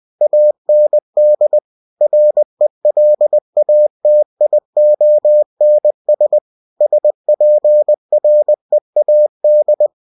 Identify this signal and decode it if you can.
Morse code, heard as a keyed tone, about 20 wpm, 600 Hz. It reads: AND RELATIONS SPREAD
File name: cw-books / winnie-the-pooh / winnie-20wpm-page-0106